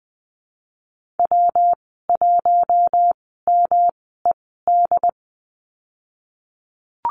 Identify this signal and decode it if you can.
Morse code, heard as a keyed tone, about 20 wpm, spaced as for 15 wpm, 700 Hz. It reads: W1MED
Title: 186 Lesson 30g - Letter b - Call Signs 20wpm